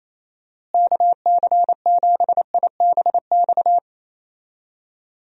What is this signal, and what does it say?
Morse code, heard as a keyed tone, about 28 wpm, 700 Hz. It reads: KC7IBX
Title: USA Call Signs - 6 Characters - Repeated 3x - 28wpm